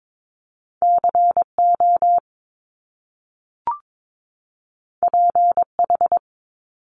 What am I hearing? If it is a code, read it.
Morse code, heard as a keyed tone, about 22 wpm, 700 Hz. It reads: CO PH
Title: Top Two Letter Combinations in Words 22wpm